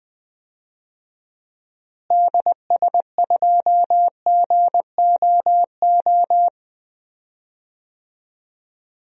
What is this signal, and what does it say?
Morse code, heard as a keyed tone, about 20 wpm, 700 Hz. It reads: DS2GOO
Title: DXCC-100 Call Signs - 6 Characters - Repeated 3x - 20wpm